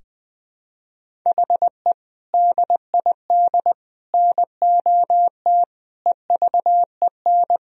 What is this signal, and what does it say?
Morse code, heard as a keyed tone, about 20 wpm, 700 Hz. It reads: HE DID NOT EVEN
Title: Sentences from Top 1000 Words - Review Part 1 - 20wpm